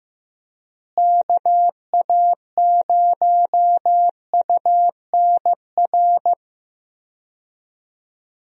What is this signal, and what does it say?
Morse code, heard as a keyed tone, about 15 wpm, 700 Hz. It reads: KA0UNR
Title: USA Call Signs - 6 Characters - Repeated 2x - 15wpm